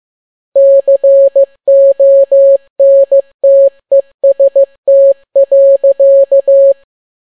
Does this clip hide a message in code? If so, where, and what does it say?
Yes: Morse code from 0.6 to 6.7 s, heard as a keyed tone, about 15 wpm, 550 Hz. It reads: CONTEST.